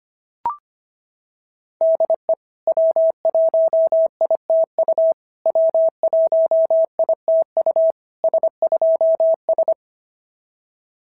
Morse code, heard as a keyed tone, about 25 wpm, 660 Hz. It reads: DE W1ITU W1ITU S2S
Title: SOTA QSO Practice 25wpm